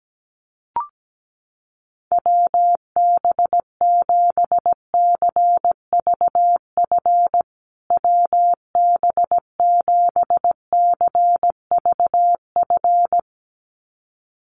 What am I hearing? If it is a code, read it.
Morse code, heard as a keyed tone, about 17 wpm, 700 Hz. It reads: WB7CVF WB7CVF
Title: USA Call Signs Repeated 2x 17wpm